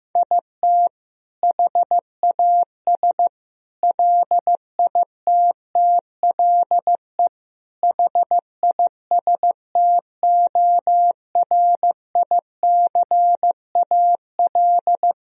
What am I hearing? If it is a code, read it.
Morse code, heard as a keyed tone, about 16 wpm, 700 Hz. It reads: IT HAS LITTLE HISTORICAL
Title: Sentences from Top 1000 Words - Review Part 4 - 15wpm